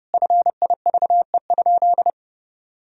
Morse code, heard as a keyed tone, about 30 wpm, 700 Hz. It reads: FIVE?
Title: Sentences from Top 1000 Words - Review Part 2 - 30wpm